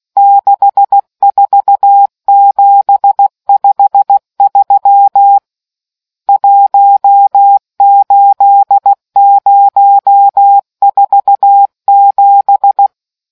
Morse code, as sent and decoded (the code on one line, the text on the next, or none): -.... ....- --... ..... ...-- / .---- ---.. ----- ....- --...
64753 18047